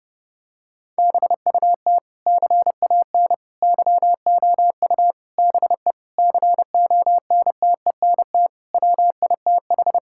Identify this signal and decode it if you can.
Morse code, heard as a keyed tone, about 28 wpm, 700 Hz. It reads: BUT CAN YOU BE CONTENT WITH
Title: Sentences from Top 1000 Words - Review Part 6 - 30wpm